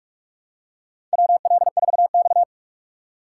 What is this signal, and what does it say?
Morse code, heard as a keyed tone, about 48 wpm, 700 Hz. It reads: WL4X
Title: USA Call Signs - 4 Characters - 2x Character Spacing - 45wpm